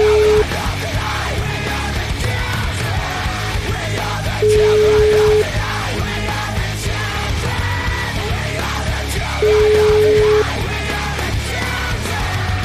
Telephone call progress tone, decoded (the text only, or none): ringback tone